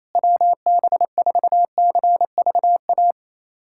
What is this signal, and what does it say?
Morse code, heard as a keyed tone, about 28 wpm, 700 Hz. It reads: WB4CVA
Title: USA Callsign DE Callsign 28wpm